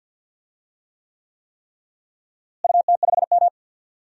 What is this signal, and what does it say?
Morse code, heard as a keyed tone, about 53 wpm, 700 Hz. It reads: UT5M
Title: DXCC-100 Call Signs - 4 Characters - Repeated 3x - 50wpm